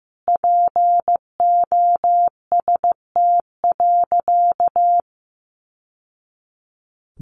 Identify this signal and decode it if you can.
Morse code, heard as a keyed tone, about 15 wpm, 700 Hz. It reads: POST.
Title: Sentences from Top 1000 Words 15wpm